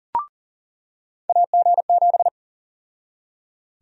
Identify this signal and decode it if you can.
Morse code, heard as a keyed tone, about 40 wpm, 700 Hz. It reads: AG7
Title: USA Call Sign Prefixes - 40wpm